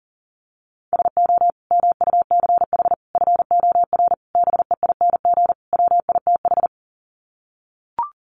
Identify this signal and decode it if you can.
Morse code, heard as a keyed tone, about 40 wpm, 700 Hz. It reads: SO MUCH FOR BEING WITH
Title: Sets of 5 Words Encore 40wpm